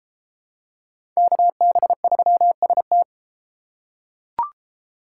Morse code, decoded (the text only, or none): KB3ST